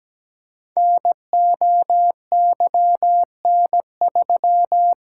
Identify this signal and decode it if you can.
Morse code, heard as a keyed tone, about 17 wpm, 690 Hz. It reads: NOYN3
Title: Code Groups - 5 Letter:Numbers - 17wpm